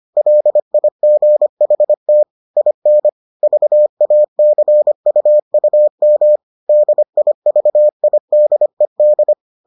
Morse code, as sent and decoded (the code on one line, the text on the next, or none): .-.. .. --. .... - / .. -. / ...- .- -.-. ..- ..- -- / -.. .. ...- .. -.. -..
LIGHT IN VACUUM DIVIDED